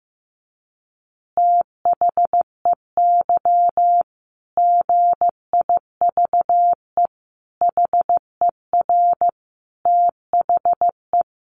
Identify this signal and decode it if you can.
Morse code, heard as a keyed tone, about 15 wpm, 700 Hz. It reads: THEY GIVE HER THE
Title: Easy Phrases 15wpm